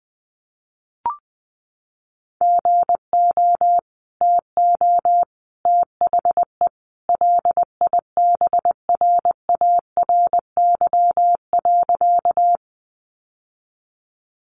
Morse code, heard as a keyed tone, about 20 wpm, 700 Hz. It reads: GO TO THE LIBRARY.
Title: Sentences from Top 600 Words 20wpm